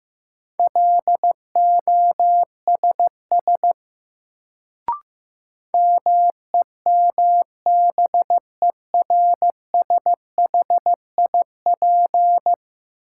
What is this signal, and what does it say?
Morse code, heard as a keyed tone, about 15 wpm, 700 Hz. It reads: LOSS MEMBERSHIP
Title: Sentences from Top 1400 Words - New Word Drill Alt - 15wpm